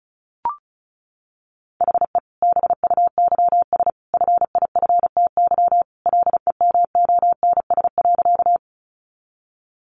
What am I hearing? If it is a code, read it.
Morse code, heard as a keyed tone, about 34 wpm, 700 Hz. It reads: HE BUYS FIFTY LEMONS.
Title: Additional Easy Phrases 35wpm